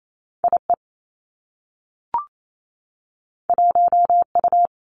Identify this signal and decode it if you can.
Morse code, heard as a keyed tone, about 28 wpm, 700 Hz. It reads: I E 1 U
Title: Two Challenging Letter-Number Combinations 28wpm